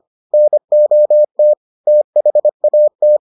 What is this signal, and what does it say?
Morse code, heard as a keyed tone, about 25 wpm, 600 Hz. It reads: NOT THAT